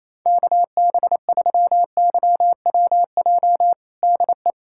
Morse code, heard as a keyed tone, about 28 wpm, 700 Hz. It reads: KB3YWJ DE
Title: USA Callsign DE Callsign 28wpm